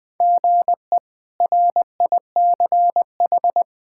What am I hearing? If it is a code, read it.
Morse code, heard as a keyed tone, about 21 wpm, 700 Hz. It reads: GE RICH
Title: K1USN SST QSO Practice - 20wpm